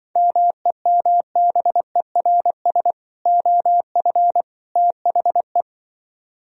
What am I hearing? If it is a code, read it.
Morse code, heard as a keyed tone, about 25 wpm, 700 Hz. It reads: MEMBERS OF THE
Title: Speed-Racing - Sets of 3 Words 20wpm